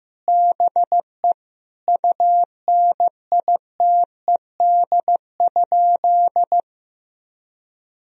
Morse code, heard as a keyed tone, about 15 wpm, 700 Hz. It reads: BE UNITED?
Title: Sentences from Top 1000 Words - Review Part 4 - 15wpm